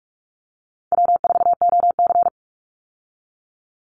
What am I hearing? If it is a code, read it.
Morse code, heard as a keyed tone, about 44 wpm, 700 Hz. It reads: W4OC
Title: Easy Call Signs Repeated 3x - 45wpm